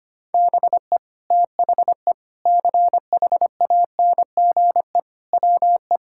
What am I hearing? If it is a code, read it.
Morse code, heard as a keyed tone, about 25 wpm, 700 Hz. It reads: BE THE CHANGE WE
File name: practice / 25wpm / Sentences with 9-Word Phrases - 25wpm